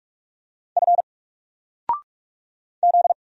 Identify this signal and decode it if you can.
Morse code, heard as a keyed tone, about 44 wpm, 700 Hz. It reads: F Z
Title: Single Character (Letter:Number:Punctuation:Prosign) 45wpm